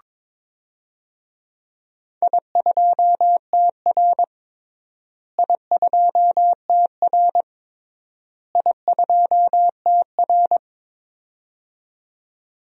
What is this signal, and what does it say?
Morse code, heard as a keyed tone, about 22 wpm, 700 Hz. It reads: I2TR I2TR I2TR
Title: DXCC-100 Call Signs - 4 Characters - Repeated 3x - 22wpm